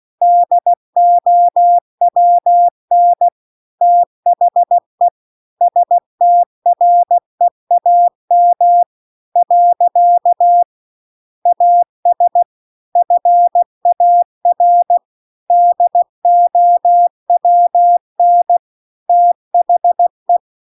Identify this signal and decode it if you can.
Morse code, heard as a keyed tone, about 16 wpm, 700 Hz. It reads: DOWN THE STREAM . AS FAR DOWN THE